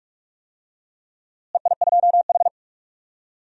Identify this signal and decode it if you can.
Morse code, heard as a keyed tone, about 42 wpm, 700 Hz. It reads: EI2H
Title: DXCC-100 Call Signs - 4 Characters - Repeated 2x - 45wpm